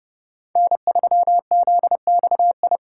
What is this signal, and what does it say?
Morse code, heard as a keyed tone, about 30 wpm, 700 Hz. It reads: N3ZXI
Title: USA Call Signs - 5 Characters - 4x Character Spacing - 30wpm